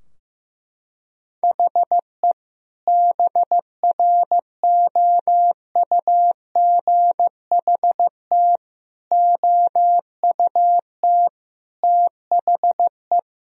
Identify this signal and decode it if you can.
Morse code, heard as a keyed tone, about 16 wpm, 700 Hz. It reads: HE BROUGHT OUT THE
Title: Sentences from Top 1000 Words - Review Part 3 - 15wpm